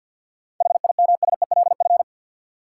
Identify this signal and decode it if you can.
Morse code, heard as a keyed tone, about 53 wpm, 700 Hz. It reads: HIMSELF
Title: Sentences from Top 1500 Words - New Word Drill - 50wpm